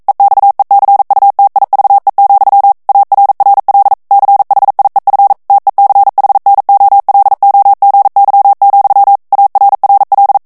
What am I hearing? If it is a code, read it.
Morse code, heard as a keyed tone, about 42 wpm, 800 Hz. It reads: EXECUTIVE, ARRL CHIEF TECHNOLOGY, ARRL
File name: 40 mots minute